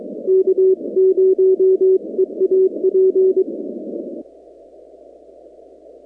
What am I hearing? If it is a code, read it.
Morse code, heard as a keyed tone, about 23 wpm, 380 Hz. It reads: K0EAP
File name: contests / 2019-cw-wpx-kt5j-k5pi- -2330 0-46